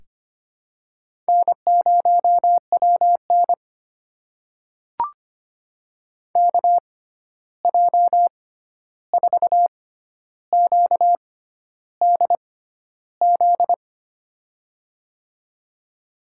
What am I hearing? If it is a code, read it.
Morse code, heard as a keyed tone, about 25 wpm, spaced as for 11 wpm, 700 Hz. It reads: N0WN KJ4QDZ
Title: USA Call Signs - 6x Character Spacing - 25wpm